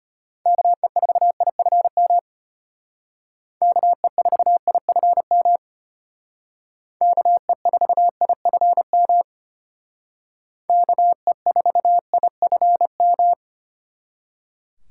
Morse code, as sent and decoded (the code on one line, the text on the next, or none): -.- . ....- .. ..-. -- / -.- . ....- .. ..-. -- / -.- . ....- .. ..-. -- / -.- . ....- .. ..-. --
KE4IFM KE4IFM KE4IFM KE4IFM